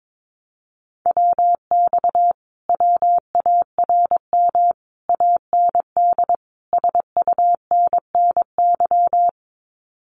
Morse code, heard as a keyed tone, about 22 wpm, 700 Hz. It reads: WX WARM AND SUNNY
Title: Common QSO Phrases Encore 22wpm